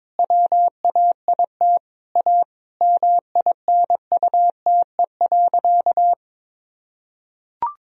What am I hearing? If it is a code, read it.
Morse code, heard as a keyed tone, about 22 wpm, 700 Hz. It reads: WAIT A MINUTE.
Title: Sentences from Top 500 Words and Abbreviations 22wpm